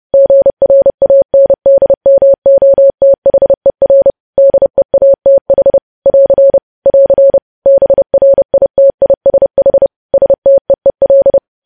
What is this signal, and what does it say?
Morse code, heard as a keyed tone, about 30 wpm, 550 Hz. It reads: GRANDMOTHER DEATH <AR> <AR> BRITISH STEEL
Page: news headlines in morse code at 30 wpm